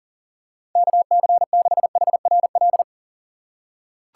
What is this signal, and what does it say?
Morse code, heard as a keyed tone, about 40 wpm, 700 Hz. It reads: KC6HRL
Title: USA Call Signs - 6 Characters - 3x Character Spacing - 40wpm